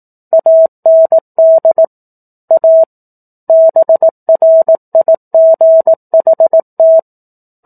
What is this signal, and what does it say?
Morse code, heard as a keyed tone, about 18 wpm, 650 Hz. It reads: AND A BRIGHT
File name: 04-Chapter-04 18 WPM